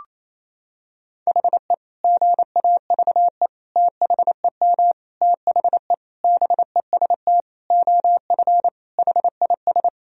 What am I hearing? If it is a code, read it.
Morse code, heard as a keyed tone, about 28 wpm, 700 Hz. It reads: HE GAVE THEM THE BEST OF HIS
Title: Sentences from Top 1000 Words - Review Part 1 - 28wpm